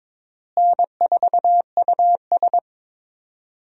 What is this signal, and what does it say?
Morse code, heard as a keyed tone, about 21 wpm, 700 Hz. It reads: N4US
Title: USA Call Signs - 4 Characters - Repeated 3x - 22wpm